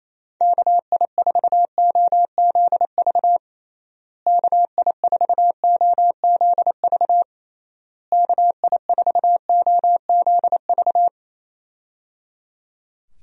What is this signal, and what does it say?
Morse code, heard as a keyed tone, about 28 wpm, 700 Hz. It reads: KI4OZV KI4OZV KI4OZV
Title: USA Call Signs - 6 Characters - Repeated 3x - 28wpm